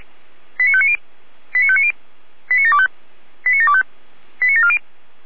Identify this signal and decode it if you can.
Five-tone selective call (ZVEI), heard as quick runs of five tones, tones of about 70 ms. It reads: 89590 89590 89725 89725 89740